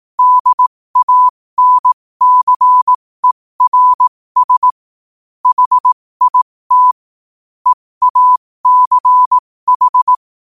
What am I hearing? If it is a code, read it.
Morse code, heard as a keyed tone, about 18 wpm, spaced as for 15 wpm, 1000 Hz. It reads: DANCERS HIT EACH
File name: qotdfiles / 15wpm